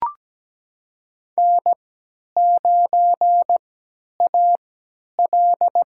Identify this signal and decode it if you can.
Morse code, heard as a keyed tone, about 17 wpm, spaced as for 10 wpm, 700 Hz. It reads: N9AL
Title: USA Call Signs - 3x Character Spacing - 17wpm